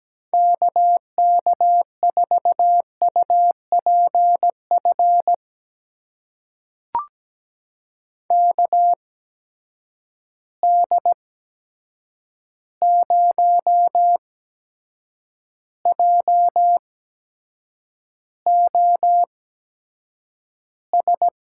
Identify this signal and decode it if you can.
Morse code, heard as a keyed tone, about 17 wpm, spaced as for 7 wpm, 700 Hz. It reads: KK4UPF KD0JOS
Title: USA Call Signs - 6 Characters - 8x Character Spacing - 17wpm